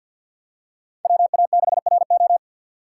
Morse code, heard as a keyed tone, about 50 wpm, 700 Hz. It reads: WA6RO
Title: USA Call Signs - 3x Character Spacing - 50wpm